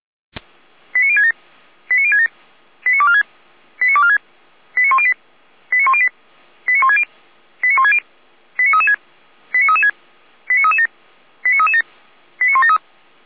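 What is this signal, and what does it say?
Five-tone selective call (ZVEI), heard as quick runs of five tones, tones of about 70 ms. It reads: 89067 89067 89246 89246 89108 89108 89160 89160 89336 89336 89337 89337 89173
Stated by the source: ZVEI five-tone selective call received by SDR.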